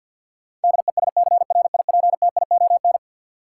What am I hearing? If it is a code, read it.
Morse code, heard as a keyed tone, about 50 wpm, 700 Hz. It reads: DESCRIPTION
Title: Sentences from Top 1300 Words - New Word Drill Alt - 50wpm